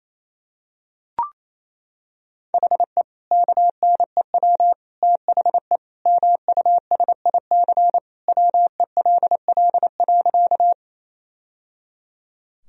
Morse code, heard as a keyed tone, about 28 wpm, 700 Hz. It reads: HE KNEW THE MUSIC WELL.
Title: Sentences from Top 1000 Words - Review Part 1 - 28wpm